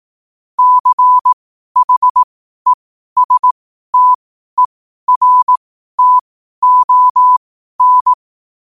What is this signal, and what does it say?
Morse code, heard as a keyed tone, about 18 wpm, spaced as for 12 wpm, 1000 Hz. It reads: CHESTERTON
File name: qotdfiles / 12wpm